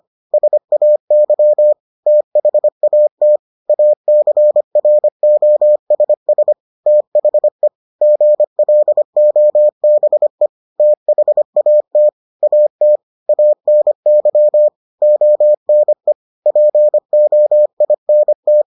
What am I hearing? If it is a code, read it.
Morse code, heard as a keyed tone, about 25 wpm, 600 Hz. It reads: SAY THAT ACROSS THE GLOBE THAT AT ANY ONE POINT